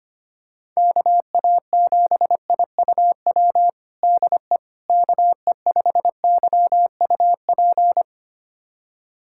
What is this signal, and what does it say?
Morse code, heard as a keyed tone, about 25 wpm, 700 Hz. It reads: KA7IUW DE KE5YUP
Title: USA Callsign DE Callsign - 6 Characters - 25wpm